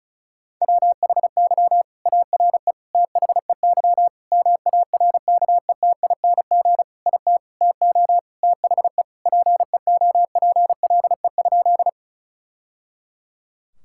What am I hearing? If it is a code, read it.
Morse code, heard as a keyed tone, about 36 wpm, 700 Hz. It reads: WHY ARE THEY MARKETING IT TO THE PEOPLE?
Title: Sentences from Top 1000 Words - Review Part 5 - 35wpm